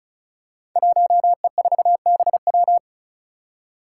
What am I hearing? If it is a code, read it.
Morse code, heard as a keyed tone, about 36 wpm, 700 Hz. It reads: 1E4BW
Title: Five Challenging Letter-Number Combinations 35wpm